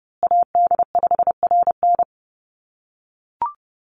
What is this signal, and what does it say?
Morse code, heard as a keyed tone, about 30 wpm, 700 Hz. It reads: AD5RN
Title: USA Call Signs Repeated 2x 30wpm